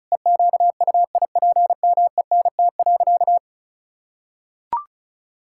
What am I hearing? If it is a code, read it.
Morse code, heard as a keyed tone, about 36 wpm, 700 Hz. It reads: EQUIPMENT.
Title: Sentences from Top 1000 Words - Review Part 5 - 35wpm